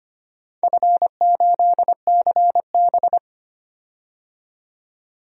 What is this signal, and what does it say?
Morse code, heard as a keyed tone, about 25 wpm, 700 Hz. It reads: F8CB
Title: DXCC-100 Call Signs - 4 Characters - 25wpm